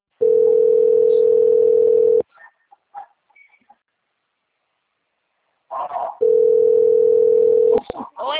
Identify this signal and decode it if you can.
Ringback tone (North America), 440 + 480 Hz.
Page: Goiania - GO